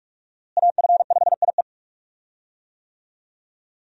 Morse code, heard as a keyed tone, about 46 wpm, 700 Hz. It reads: AF5IE
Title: USA Call Signs - 5 Characters - Repeated 3x - 45wpm